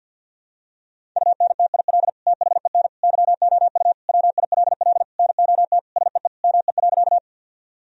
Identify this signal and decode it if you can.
Morse code, heard as a keyed tone, about 48 wpm, 700 Hz. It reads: UNTIL THEN YOU WILL NOT SEE ME.
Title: Sentences from Top 1000 Words - Review Part 1 - 50wpm